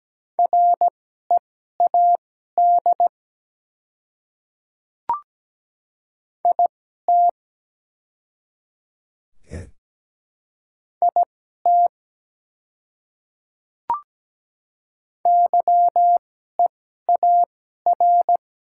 Morse code, heard as a keyed tone, about 17 wpm, spaced as for 12 wpm, 700 Hz. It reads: READ IT IT YEAR
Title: 2x Character Spacing - Top 200 Words - 17wpm